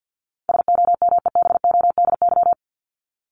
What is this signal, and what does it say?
Morse code, heard as a keyed tone, about 53 wpm, 700 Hz. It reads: SOMEBODY
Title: Top 1751-2000 Words 50wpm